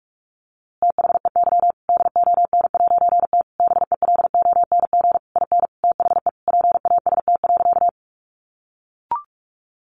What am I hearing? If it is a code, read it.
Morse code, heard as a keyed tone, about 46 wpm, 700 Hz. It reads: THEY DON'T BELONG IN THE PAST.
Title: A Sound Of Thunder 45wpm